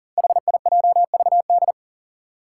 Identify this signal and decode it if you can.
Morse code, heard as a keyed tone, about 41 wpm, 700 Hz. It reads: HIJVD
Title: Five Challenging Letter-Number Combinations 40wpm